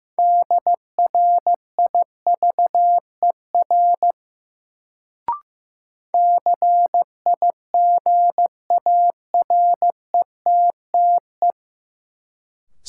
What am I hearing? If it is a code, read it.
Morse code, heard as a keyed tone, about 15 wpm, 700 Hz. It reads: DRIVER CIGARETTE
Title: Sentences from Top 1000 Words - New Word Drill - 15wpm